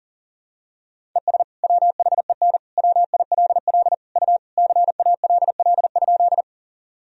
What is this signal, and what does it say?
Morse code, heard as a keyed tone, about 40 wpm, 700 Hz. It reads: ES WHEN WILL U CALL?
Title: Sentences from Top 100 Words and Abbreviations 40wpm